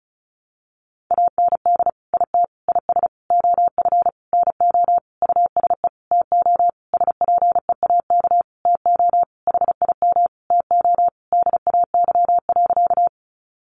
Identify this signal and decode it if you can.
Morse code, heard as a keyed tone, about 34 wpm, 700 Hz. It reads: AND IT IS OF NO USE TO SPEAK TO HIM TO DAY.
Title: Sentences from Top 1100 Words 35wpm